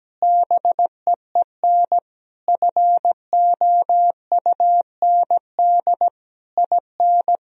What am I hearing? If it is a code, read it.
Morse code, heard as a keyed tone, about 17 wpm, 700 Hz. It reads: BEEN FOUND IN
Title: Sentences from Top 1000 Words - Review Part 3 - 17wpm